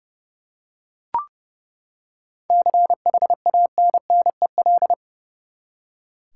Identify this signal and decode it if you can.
Morse code, heard as a keyed tone, about 32 wpm, 700 Hz. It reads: CHANNEL
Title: Seven Letter Words - 30wpm